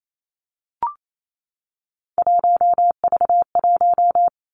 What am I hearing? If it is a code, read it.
Morse code, heard as a keyed tone, about 28 wpm, 700 Hz. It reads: 1V1
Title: Three Challenging Letter-Number Combinations 28wpm